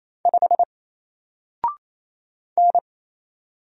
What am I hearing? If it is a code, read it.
Morse code, heard as a keyed tone, about 28 wpm, 700 Hz. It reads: 5 N
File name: Single-Letter-Number-28wpm